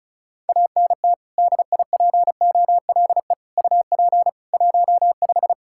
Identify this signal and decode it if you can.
Morse code, heard as a keyed tone, about 36 wpm, 700 Hz. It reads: ANT DIPOLE UP 15
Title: Practice QSO - Exchange 2 - Info - 35wpm